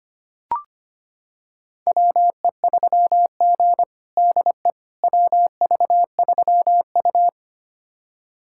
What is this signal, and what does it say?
Morse code, heard as a keyed tone, about 25 wpm, 700 Hz. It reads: WE3G DE WV3U
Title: USA Callsign DE Callsign - 4 Characters - 25wpm